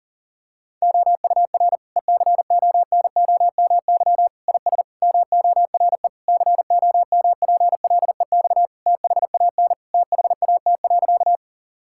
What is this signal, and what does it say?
Morse code, heard as a keyed tone, about 40 wpm, 700 Hz. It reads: OUR ECONOMY IS MORE COMPLEX THAN THAT.
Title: Sentences from Top 1000 Words - Review Part 6 - 40wpm